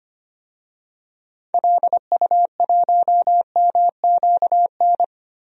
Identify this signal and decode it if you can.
Morse code, heard as a keyed tone, about 25 wpm, 700 Hz. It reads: LU1MQN
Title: DXCC-100 Call Signs - 6 Characters - 25wpm